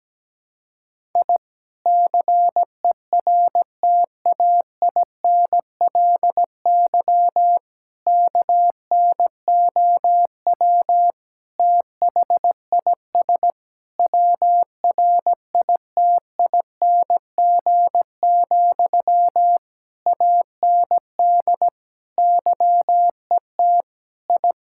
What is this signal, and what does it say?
Morse code, heard as a keyed tone, about 17 wpm, 700 Hz. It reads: I CERTAINLY KNOW THIS WRITING, AND YET I
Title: Sentences from Top 1000 Words - Review Part 6 - 17wpm